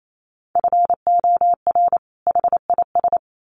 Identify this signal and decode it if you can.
Morse code, heard as a keyed tone, about 28 wpm, 700 Hz. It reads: FOR HIS
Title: Sets of 2 Words 28wpm